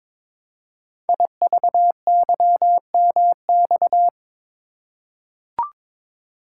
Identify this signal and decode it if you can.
Morse code, heard as a keyed tone, about 22 wpm, 700 Hz. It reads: IVYMX
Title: Code Groups - 5 Letters - 22wpm